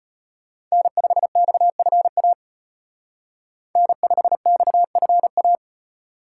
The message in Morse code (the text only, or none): N5XFA N5XFA